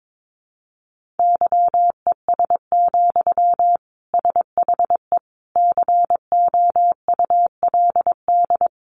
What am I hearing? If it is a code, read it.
Morse code, heard as a keyed tone, about 22 wpm, 700 Hz. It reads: YES, SHE COULD
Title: Sentences from Top 500 Words 22wpm